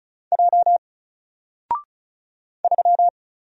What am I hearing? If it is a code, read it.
Morse code, heard as a keyed tone, about 34 wpm, 700 Hz. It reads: J 3
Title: Single Character (Letter:Number:Punctuation:Prosign) Alt 35wpm